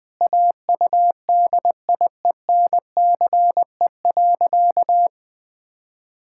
Morse code, heard as a keyed tone, about 21 wpm, 700 Hz. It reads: AUDIENCE.
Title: Pangram Challenge v2 - 20wpm